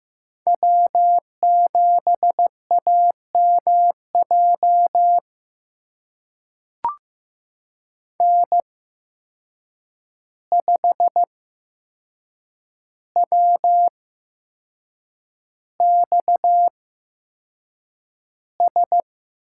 Morse code, heard as a keyed tone, about 15 wpm, spaced as for 6 wpm, 700 Hz. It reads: W7AMJ N5WXS
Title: USA Call Signs - 5 Characters - 8x Character Spacing - 15wpm